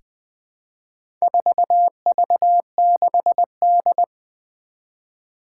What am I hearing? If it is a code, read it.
Morse code, heard as a keyed tone, about 20 wpm, 700 Hz. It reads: 4V6D
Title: Four Challenging Letter-Number Combinations 20wpm